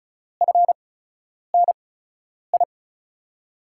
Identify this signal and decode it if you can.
Morse code, heard as a keyed tone, about 36 wpm, spaced as for 10 wpm, 700 Hz. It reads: FNI
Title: USA Call Signs - 8x Character Spacing - 35wpm